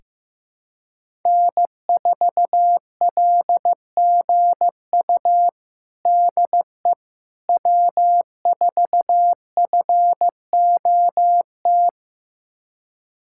Morse code, heard as a keyed tone, about 15 wpm, 700 Hz. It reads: N4LGU DE W4FOT
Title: USA Callsign DE Callsign - 5 Characters - 15wpm